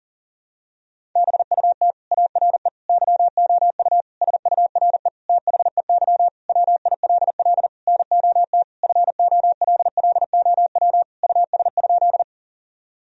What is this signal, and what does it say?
Morse code, heard as a keyed tone, about 37 wpm, 700 Hz. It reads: BUT ARE YOU SURE THEY WILL NOT FOLLOW US?
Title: Sentences from Top 1000 Words - Review Part 1 - 40wpm